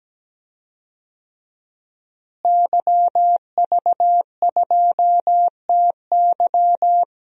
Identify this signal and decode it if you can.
Morse code, heard as a keyed tone, about 17 wpm, 700 Hz. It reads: YV2TY
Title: DXCC-100 Call Signs - 5 Characters - Repeated 3x - 17wpm